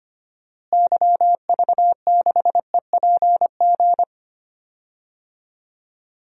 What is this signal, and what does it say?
Morse code, heard as a keyed tone, about 25 wpm, 700 Hz. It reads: YV6EPG
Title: DXCC-100 Call Signs - 6 Characters - Repeated 2x - 25wpm